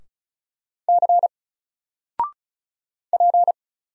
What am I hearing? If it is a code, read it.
Morse code, heard as a keyed tone, about 34 wpm, 700 Hz. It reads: C P
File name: Single-Letter-Number-35wpm